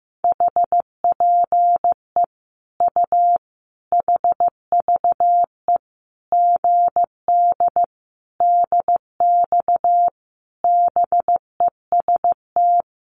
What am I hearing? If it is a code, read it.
Morse code, heard as a keyed tone, about 15 wpm, 700 Hz. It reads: HPE U HVE GD DX BEST
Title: Common QSO Phrases 15wpm